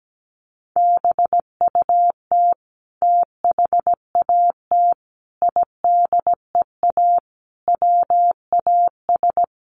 Morse code, heard as a keyed tone, about 17 wpm, 700 Hz. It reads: BUT THAT IDEA WAS
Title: Sentences from Top 900 Words 17wpm